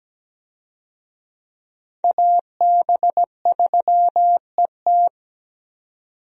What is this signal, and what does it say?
Morse code, heard as a keyed tone, about 17 wpm, 700 Hz. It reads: AB3ET
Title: USA Call Signs - 5 Characters - 17wpm